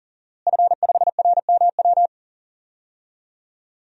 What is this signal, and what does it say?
Morse code, heard as a keyed tone, about 40 wpm, 700 Hz. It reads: F5RMW
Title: DXCC-100 Call Signs - 5 Characters - Repeated 3x - 40wpm